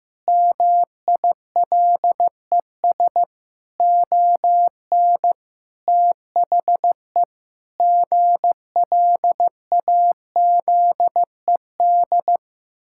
Morse code, heard as a keyed tone, about 14 wpm, 700 Hz. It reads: MILES ON THE GLAZED
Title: Pangram Challenge v2 - 15wpm